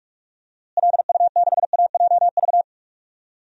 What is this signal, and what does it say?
Morse code, heard as a keyed tone, about 44 wpm, 700 Hz. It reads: LU6AJV
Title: DXCC-100 Call Signs - 6 Characters - Repeated 2x - 45wpm